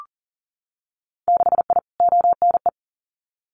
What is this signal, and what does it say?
Morse code, heard as a keyed tone, about 42 wpm, 700 Hz. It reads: 6I ONE
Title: Winter Field Day Exchanges Encore 40wpm